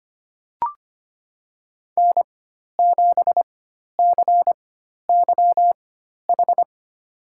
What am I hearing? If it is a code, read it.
Morse code, heard as a keyed tone, about 25 wpm, spaced as for 12 wpm, 700 Hz. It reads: N7CYH
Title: USA Call Signs - 4x Character Spacing - 25wpm